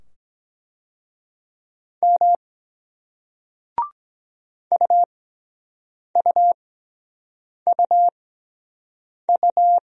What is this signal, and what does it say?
Morse code, heard as a keyed tone, about 24 wpm, spaced as for 7 wpm, 700 Hz. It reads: M UUUU